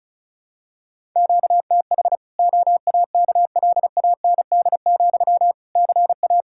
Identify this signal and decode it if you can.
Morse code, heard as a keyed tone, about 36 wpm, 700 Hz. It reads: QTH OAKLAND, CA
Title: Practice QSO - Exchange 1 - Beginning - 35wpm